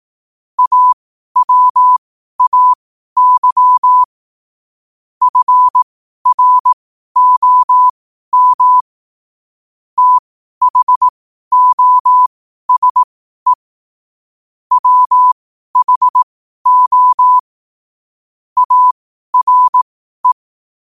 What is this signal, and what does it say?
Morse code, heard as a keyed tone, about 18 wpm, spaced as for 12 wpm, 1000 Hz. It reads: AWAY FROM THOSE WHO ARE
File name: qotdfiles / 12wpm